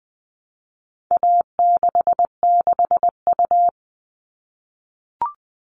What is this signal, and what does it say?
Morse code, heard as a keyed tone, about 20 wpm, 700 Hz. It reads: A66U
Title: All Cut Numbers - 4-Digit 20wpm